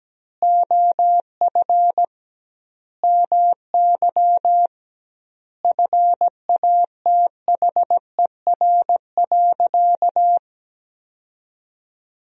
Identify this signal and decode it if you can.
Morse code, heard as a keyed tone, about 17 wpm, 700 Hz. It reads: OF MY FATHER.
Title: 2x Word-Spacing - Sentences from Top 500 Words - 17wpm